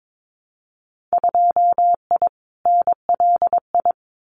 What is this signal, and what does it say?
Morse code, heard as a keyed tone, about 22 wpm, 700 Hz. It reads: 2I NLI
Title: Winter Field Day Exchanges 22wpm